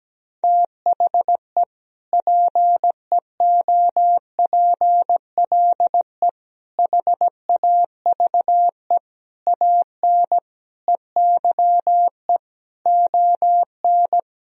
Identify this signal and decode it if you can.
Morse code, heard as a keyed tone, about 17 wpm, 700 Hz. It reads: THE PEOPLE HAVE AN EYE ON
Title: Sentences from Top 1000 Words - Review Part 1 - 17wpm